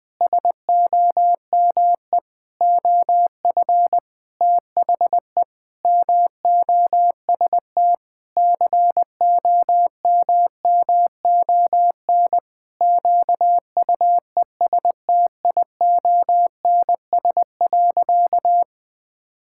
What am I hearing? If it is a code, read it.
Morse code, heard as a keyed tone, about 20 wpm, 700 Hz. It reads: SOME OF THE MOST COMMON QUESTIONS.
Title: Sentences with 4-Word Phrases - 20wpm